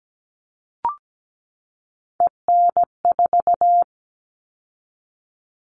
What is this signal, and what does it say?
Morse code, heard as a keyed tone, about 17 wpm, 700 Hz. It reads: EN4
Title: All Cut Numbers - 3-Digit 17wpm